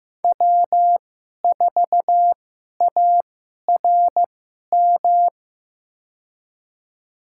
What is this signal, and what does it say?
Morse code, heard as a keyed tone, about 15 wpm, spaced as for 11 wpm, 700 Hz. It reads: W4ARM
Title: USA Call Signs - 5 Characters - 2x Character Spacing - 15wpm